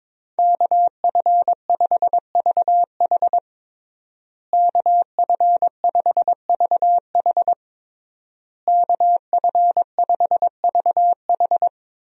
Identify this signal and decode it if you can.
Morse code, heard as a keyed tone, about 22 wpm, 700 Hz. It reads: KF5VH KF5VH KF5VH
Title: USA Call Signs - 5 Characters - Repeated 3x - 22wpm